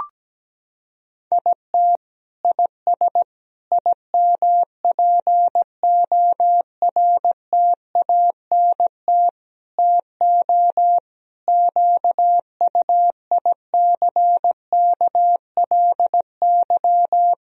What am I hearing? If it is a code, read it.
Morse code, heard as a keyed tone, about 17 wpm, 700 Hz. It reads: IT IS IMPORTANT TO QUICKLY
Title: Sentences from Top 1000 Words - Review Part 4 - 17wpm